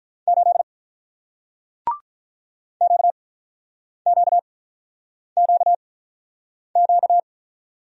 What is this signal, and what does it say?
Morse code, heard as a keyed tone, about 47 wpm, spaced as for 9 wpm, 700 Hz. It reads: 8 QQQQ